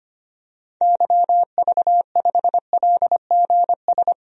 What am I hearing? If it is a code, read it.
Morse code, heard as a keyed tone, about 25 wpm, 700 Hz. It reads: YV5LGS
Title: DXCC-100 Call Signs - 6 Characters - 25wpm